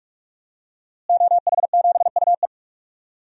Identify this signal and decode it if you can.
Morse code, heard as a keyed tone, about 46 wpm, 700 Hz. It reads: OH7UE
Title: DXCC-100 Call Signs - 5 Characters - Repeated 3x - 45wpm